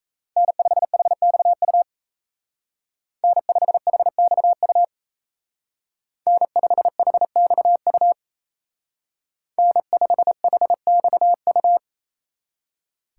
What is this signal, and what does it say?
Morse code, heard as a keyed tone, about 36 wpm, 700 Hz. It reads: N5HXU N5HXU N5HXU N5HXU